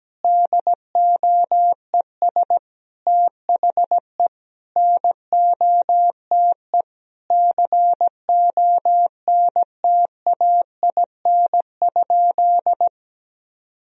Morse code, heard as a keyed tone, about 17 wpm, 700 Hz. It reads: DOES THE NOTE CONTAIN?
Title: Sentences from Top 1000 Words - Review Part 2 - 17wpm